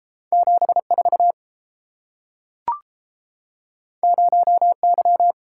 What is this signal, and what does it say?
Morse code, heard as a keyed tone, about 30 wpm, 700 Hz. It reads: 74 0Y